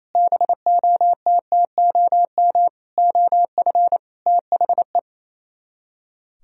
Morse code, heard as a keyed tone, about 33 wpm, 700 Hz. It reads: BOTTOM OF THE
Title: Speed-Racing - Sets of 5 Words 28wpm